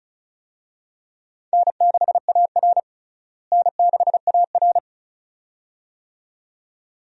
Morse code, heard as a keyed tone, about 34 wpm, 700 Hz. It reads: N6AR N6AR
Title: USA Call Signs - 4 Characters - Repeated 2x - 35wpm